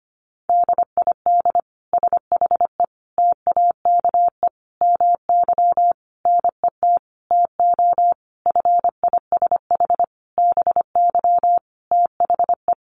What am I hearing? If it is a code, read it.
Morse code, heard as a keyed tone, about 25 wpm, 700 Hz. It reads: DID SHE TAKE MY NET TO FISH BY THE
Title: Sentences from Top 1400 Words 25wpm